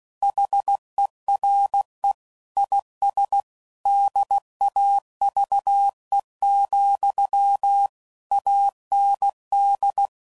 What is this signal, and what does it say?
Morse code, heard as a keyed tone, about 16 wpm, 800 Hz. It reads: HERE IS DAVE, AND
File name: CW Code Groups @15WPM / QSOs / kc7sac